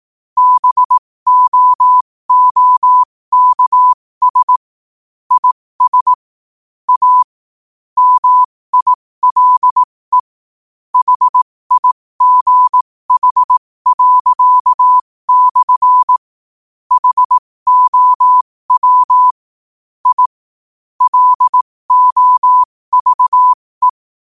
Morse code, heard as a keyed tone, about 18 wpm, spaced as for 15 wpm, 1000 Hz. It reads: BOOKS IS A MILE HIGH./ HOW I LOVE
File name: qotdfiles / 15wpm